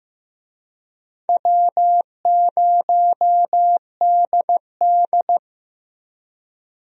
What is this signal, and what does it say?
Morse code, heard as a keyed tone, about 15 wpm, 700 Hz. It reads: W0DD
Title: USA Call Signs - 4 Characters - Repeated 2x - 15wpm